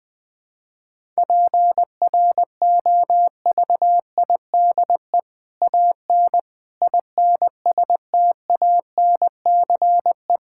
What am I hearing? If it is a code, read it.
Morse code, heard as a keyed tone, about 19 wpm, 700 Hz. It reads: PROVIDE AN INSTANCE
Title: Sentences from Top 1000 Words - Review Part 5 - 20wpm